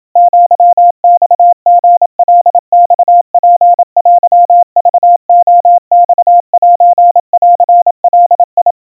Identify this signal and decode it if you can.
Morse code (Japanese Wabun code), heard as a keyed tone, about 27 wpm, 700 Hz. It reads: アマリカマツテクレマセンガ
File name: kawa / Morse sample / kamyu 100 cpm 700 Hz Crispy